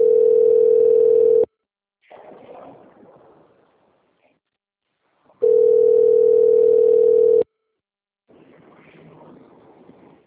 Ringback tone (North America), 440 + 480 Hz.
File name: rg-1600-unknown- - - .10142